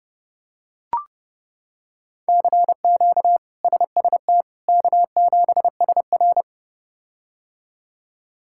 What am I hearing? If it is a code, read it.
Morse code, heard as a keyed tone, about 31 wpm, 700 Hz. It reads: CQ SST K7SR
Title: K1USN SST QSO Practice - 30wpm